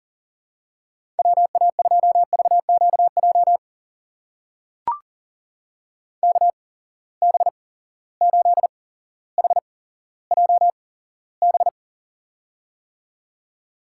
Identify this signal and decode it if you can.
Morse code, heard as a keyed tone, about 42 wpm, spaced as for 17 wpm, 700 Hz. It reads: WA2VQJ KB8HJB
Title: USA Call Signs - 6 Characters - 8x Character Spacing - 40wpm